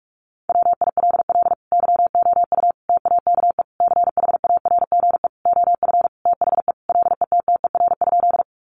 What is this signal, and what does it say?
Morse code, heard as a keyed tone, about 44 wpm, 700 Hz. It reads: WILL YOU TAKE CHARGE OF THE LETTER?
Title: Sentences from Top 900 Words 45wpm